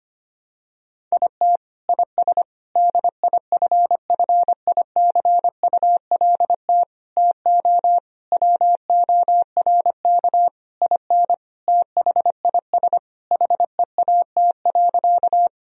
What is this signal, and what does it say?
Morse code, heard as a keyed tone, about 25 wpm, 700 Hz. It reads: IT IS DIFFICULT TO WORK IN THIS HEAT.
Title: Sentences from Top 1000 Words - Review Part 3 - 25wpm